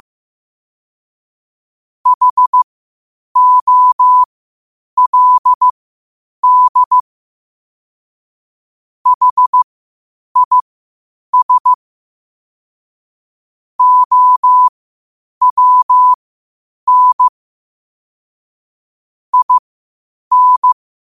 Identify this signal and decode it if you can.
Morse code, heard as a keyed tone, about 15 wpm, spaced as for 8 wpm, 1000 Hz. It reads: HOLD HIS OWN IN